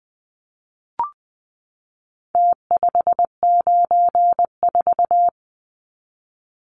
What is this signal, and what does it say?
Morse code, heard as a keyed tone, about 20 wpm, 700 Hz. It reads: T594
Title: Numbers - Contest Style - 4-Digit 20wpm